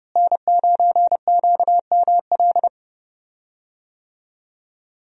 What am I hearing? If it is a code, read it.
Morse code, heard as a keyed tone, about 30 wpm, 700 Hz. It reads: N9QML
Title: USA Call Signs - 5 Characters - Repeated 2x - 30wpm